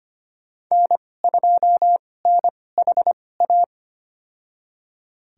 Morse code, heard as a keyed tone, about 25 wpm, spaced as for 18 wpm, 700 Hz. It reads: N2NHA
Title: USA Call Signs - 2x Character Spacing - 25wpm